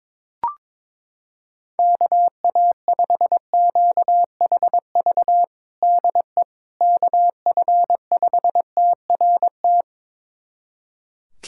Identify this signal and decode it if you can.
Morse code, heard as a keyed tone, about 22 wpm, 700 Hz. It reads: KA5QHV DE KF5TRT
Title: USA Callsign DE Callsign - 6 Characters - 22wpm